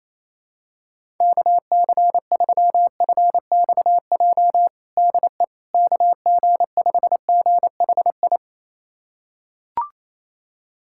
Morse code, heard as a keyed tone, about 28 wpm, 700 Hz. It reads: KC3FXJ DE KG5GHI
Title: USA Callsign DE Callsign 28wpm